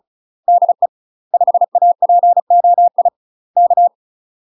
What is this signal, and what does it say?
Morse code, heard as a keyed tone, about 34 wpm, 700 Hz. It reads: DE 5APOI K
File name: UK MASTER QSOs TRAINING / 35WPM / UK MASTER QSOs 0005